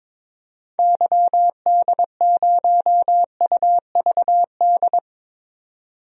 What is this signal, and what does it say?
Morse code, heard as a keyed tone, about 21 wpm, 700 Hz. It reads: YD0UVD
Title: DXCC-100 Call Signs - 6 Characters - 22wpm